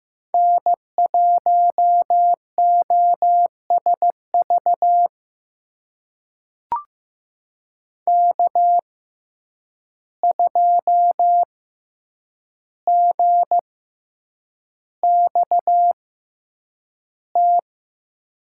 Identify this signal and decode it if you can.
Morse code, heard as a keyed tone, about 15 wpm, spaced as for 8 wpm, 700 Hz. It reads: N1OSV K2GXT
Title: USA Call Signs - 5 Characters - 6x Character Spacing - 15wpm